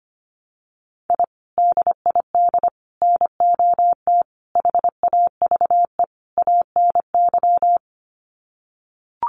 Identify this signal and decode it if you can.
Morse code, heard as a keyed tone, about 25 wpm, 700 Hz. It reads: I DID NOT HAVE ANY
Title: Sets of 5 Words Encore 25wpm